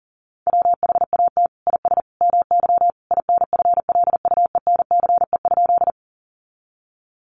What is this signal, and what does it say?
Morse code, heard as a keyed tone, about 40 wpm, 700 Hz. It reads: WHAT IS MY INFLUENCE?
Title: Sentences from Top 1100 Words 40wpm